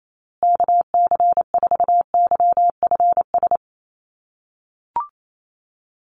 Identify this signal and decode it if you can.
Morse code, heard as a keyed tone, about 28 wpm, 700 Hz. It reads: KC4YFS
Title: USA Call Signs Repeated 3x 28wpm